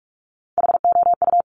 Morse code, heard as a keyed tone, about 46 wpm, 700 Hz. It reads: HOU
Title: Top Three Letter Combinations in Words 45wpm